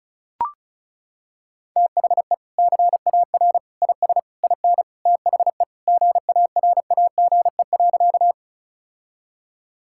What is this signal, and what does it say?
Morse code, heard as a keyed tone, about 36 wpm, 700 Hz. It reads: THE CAR IS IN THE GARAGE.
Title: Sentences from Top 1600 Words 35wpm